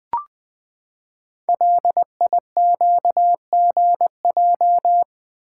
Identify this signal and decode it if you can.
Morse code, heard as a keyed tone, about 20 wpm, 700 Hz. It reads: LIQGJ
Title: Code Groups - 5 Letters - 20wpm